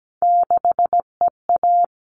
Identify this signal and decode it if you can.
Morse code, heard as a keyed tone, about 17 wpm, 700 Hz. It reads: 6EA
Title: All Cut Numbers - 3-Digit 17wpm